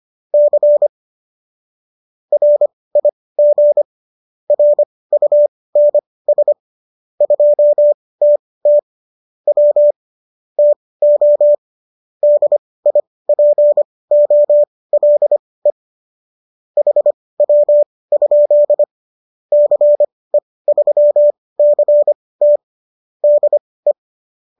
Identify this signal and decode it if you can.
Morse code, heard as a keyed tone, about 25 wpm, spaced as for 18 wpm, 590 Hz. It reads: C RIG RUNS 2TT W TO DIPOLE HW? CE3CT DE